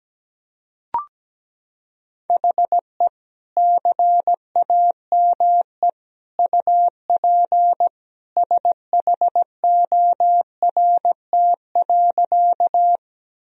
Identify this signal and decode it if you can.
Morse code, heard as a keyed tone, about 17 wpm, 700 Hz. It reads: HE CAME UP SHORT.
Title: Sentences from Top 1000 Words - Review Part 2 - 17wpm